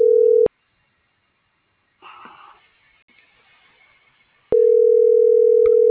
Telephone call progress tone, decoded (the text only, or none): ringback tone